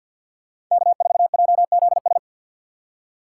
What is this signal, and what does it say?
Morse code, heard as a keyed tone, about 46 wpm, 690 Hz. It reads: K4JZS
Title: USA Call Signs - 5 Characters - 3x Character Spacing - 50wpm